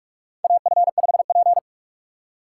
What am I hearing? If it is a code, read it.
Morse code, heard as a keyed tone, about 46 wpm, 700 Hz. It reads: AF5P
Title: K1USN SST QSO Practice - 45wpm